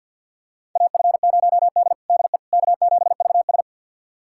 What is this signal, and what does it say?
Morse code, heard as a keyed tone, about 50 wpm, 700 Hz. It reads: AF0D DE K7VS
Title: USA Callsign DE Callsign - 4 Characters - 50wpm